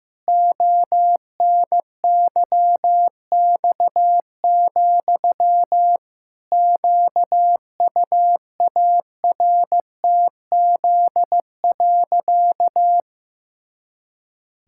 Morse code, heard as a keyed tone, about 15 wpm, 700 Hz. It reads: ONYX, QUARTZ.
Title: Pangram Challenge v2 - 15wpm